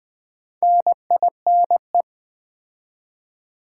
Morse code, heard as a keyed tone, about 20 wpm, 700 Hz. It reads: NINE
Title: Build A Word - 2 Letters At A Time - 8-Letter Words - 20wpm